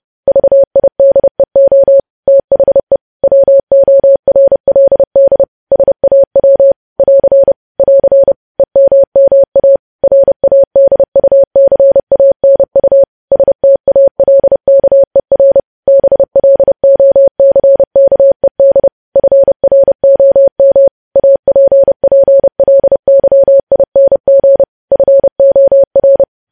Morse code, heard as a keyed tone, about 30 wpm, 550 Hz. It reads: VIDEO THE WORLD SAW <AR> <AR> EMMA RADUCANU STALKER BLOCKED FROM APPLYING FOR
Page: news headlines in morse code at 30 wpm